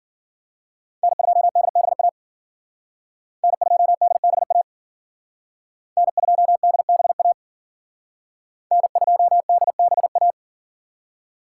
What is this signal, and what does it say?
Morse code, heard as a keyed tone, about 50 wpm, 700 Hz. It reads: N2DBA N2DBA N2DBA N2DBA
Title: Speed Racing - USA 1x3 Call Signs 40wpm